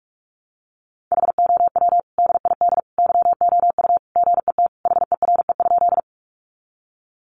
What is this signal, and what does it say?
Morse code, heard as a keyed tone, about 44 wpm, 700 Hz. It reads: HOW DID YOU GET HERE?
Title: Sentences from Top 200 Words 45wpm